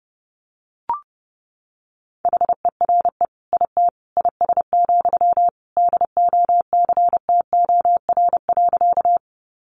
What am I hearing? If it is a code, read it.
Morse code, heard as a keyed tone, about 30 wpm, 700 Hz. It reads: HERE IT IS, DOCTOR.
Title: Sentences from Top 1500 Words 30wpm